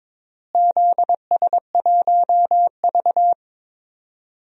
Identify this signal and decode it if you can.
Morse code, heard as a keyed tone, about 22 wpm, 700 Hz. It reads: ZS1V
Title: DXCC-100 Call Signs - 4 Characters - Repeated 3x - 22wpm